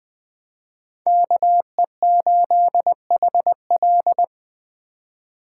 Morse code, heard as a keyed tone, about 20 wpm, 700 Hz. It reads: KE8HL
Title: Speed Racing - USA Call Signs 17wpm